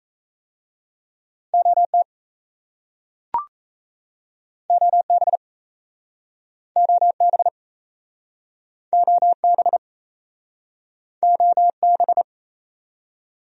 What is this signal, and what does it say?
Morse code, heard as a keyed tone, about 37 wpm, 700 Hz. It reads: OT OB OB OB OB